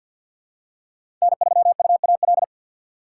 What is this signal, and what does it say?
Morse code, heard as a keyed tone, about 45 wpm, 700 Hz. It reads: N3UAL
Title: USA Call Signs - 5 Characters - 2x Character Spacing - 50wpm